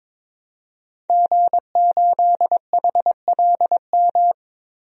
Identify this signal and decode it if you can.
Morse code, heard as a keyed tone, about 22 wpm, 700 Hz. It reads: G8HLM
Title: DXCC-100 Call Signs - 5 Characters - 22wpm